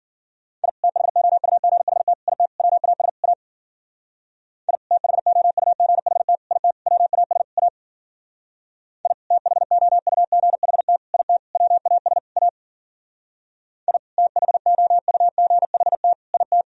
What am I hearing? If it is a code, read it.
Morse code, heard as a keyed tone, about 48 wpm, 700 Hz. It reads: I THOUGHT IT WAS A I THOUGHT IT WAS A I THOUGHT IT WAS A I THOUGHT IT
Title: Speed-Racing - Sets of 5 Words 40wpm